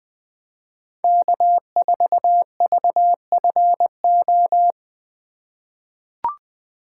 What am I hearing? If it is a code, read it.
Morse code, heard as a keyed tone, about 20 wpm, 700 Hz. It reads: K4VFO
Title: USA Call Signs - 3x Character Spacing - 20wpm